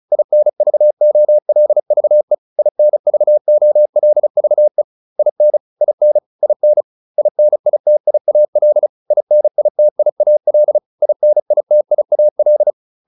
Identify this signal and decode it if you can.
Morse code, heard as a keyed tone, about 34 wpm, 600 Hz. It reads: INVOLVE INVOLVE IN IN IN INITIAL INITIAL INITIAL
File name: IN7 35